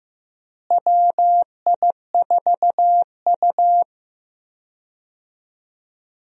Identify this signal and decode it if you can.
Morse code, heard as a keyed tone, about 15 wpm, 700 Hz. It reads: WI4U
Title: USA Call Signs - 4 Characters - Repeated 3x - 15wpm